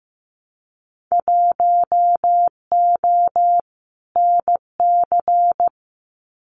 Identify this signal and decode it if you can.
Morse code, heard as a keyed tone, about 15 wpm, 700 Hz. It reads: 1O NC
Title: Winter Field Day Exchanges 15wpm